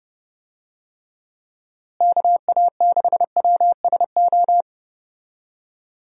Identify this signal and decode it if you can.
Morse code, heard as a keyed tone, about 30 wpm, 700 Hz. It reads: KA6WSO
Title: USA Call Signs - 6 Characters - Repeated 2x - 30wpm